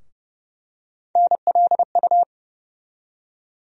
Morse code, heard as a keyed tone, about 30 wpm, 700 Hz. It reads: NLU
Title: USA Call Sign Postfixes - 3 Letters 30wpm